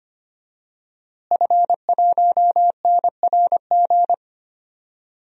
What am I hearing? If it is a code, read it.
Morse code, heard as a keyed tone, about 25 wpm, 700 Hz. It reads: F1NRG
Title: DXCC-100 Call Signs - 5 Characters - Repeated 3x - 25wpm